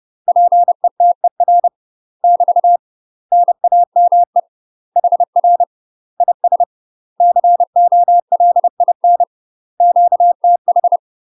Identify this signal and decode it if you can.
Morse code, heard as a keyed tone, about 30 wpm, 700 Hz. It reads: PETER <BT> NAME HR IS COLIN QTH
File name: UK MASTER QSOs TRAINING / 30WPM / UK MASTER QSOs 0007